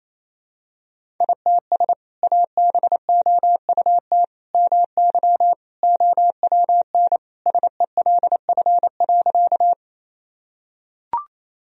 Morse code, heard as a keyed tone, about 28 wpm, 700 Hz. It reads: ITS ABOUT MY OWN SELF.
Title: Sentences from Top 1000 Words - Review Part 1 - 28wpm